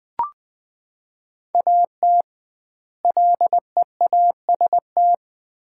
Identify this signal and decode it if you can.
Morse code, heard as a keyed tone, about 21 wpm, 700 Hz. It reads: AT LEAST
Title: 2x Word-Spacing - Sets of 2 Words - 20wpm